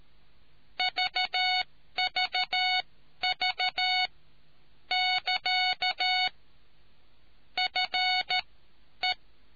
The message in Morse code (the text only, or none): VVV <KA> FE